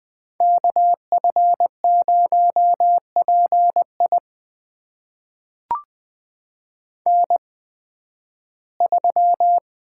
Morse code, heard as a keyed tone, about 20 wpm, spaced as for 13 wpm, 700 Hz. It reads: KF0PI N3
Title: USA Call Signs - 5 Characters - 8x Character Spacing - 20wpm